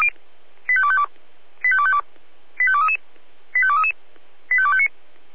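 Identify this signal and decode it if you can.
Five-tone selective call (ZVEI), heard as quick runs of five tones, tones of about 70 ms. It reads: 85252 85252 85233 85233 85259
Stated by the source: ZVEI five-tone call received by SDR, logged by fire brigade Feuerwehr Orlamünde (f85360).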